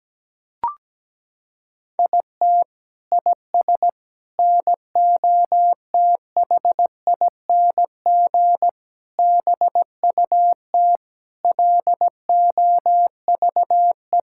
Morse code, heard as a keyed tone, about 17 wpm, 700 Hz. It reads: IT IS NOTHING BUT LOVE
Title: Sentences from Top 1600 Words 17wpm